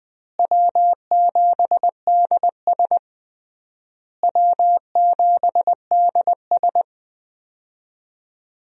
Morse code, heard as a keyed tone, about 21 wpm, 690 Hz. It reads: W7DS W7DS
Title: USA Call Signs - 4 Characters - Repeated 3x - 20wpm